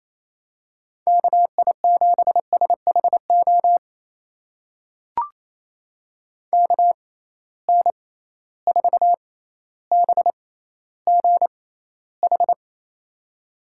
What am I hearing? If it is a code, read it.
Morse code, heard as a keyed tone, about 28 wpm, spaced as for 14 wpm, 700 Hz. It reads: KI7SHO KN4BGH
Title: USA Call Signs - 6 Characters - 6x Character Spacing - 28wpm